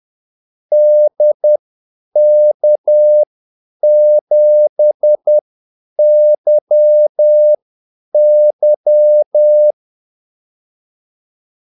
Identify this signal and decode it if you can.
Morse code, heard as a keyed tone, about 10 wpm, spaced as for 8 wpm, 600 Hz. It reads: DK7YY
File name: Callsigns-10-8wpm